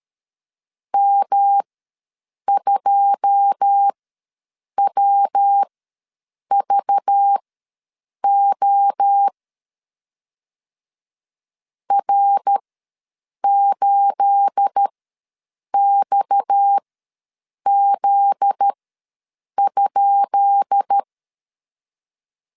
Morse code, heard as a keyed tone, about 13 wpm, spaced as for 7 wpm, 800 Hz. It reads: M2WVO R8XZ?
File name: LEZIONI / 10 SIGNS / 10B INT ws 12wpm 3527